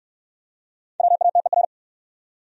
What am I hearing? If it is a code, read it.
Morse code, heard as a keyed tone, about 64 wpm, 700 Hz. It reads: WATER